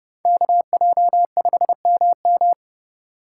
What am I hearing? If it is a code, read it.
Morse code, heard as a keyed tone, about 29 wpm, 700 Hz. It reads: KJ5MM